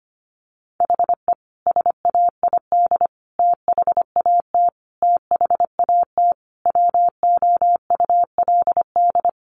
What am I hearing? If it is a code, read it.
Morse code, heard as a keyed tone, about 25 wpm, 700 Hz. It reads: HE SAID THAT THAT WOULD
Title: Sentences from Top 200 Words 25wpm